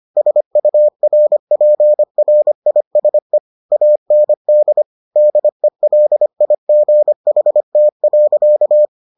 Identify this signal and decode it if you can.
Morse code, heard as a keyed tone, about 25 wpm, 600 Hz. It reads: SURPRISE AND DELIGHT.